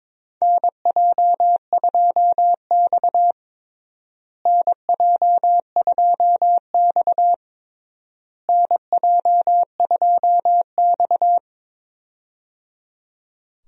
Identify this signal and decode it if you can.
Morse code, heard as a keyed tone, about 21 wpm, 700 Hz. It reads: NJ2X NJ2X NJ2X
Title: USA Call Signs - 4 Characters - Repeated 3x - 22wpm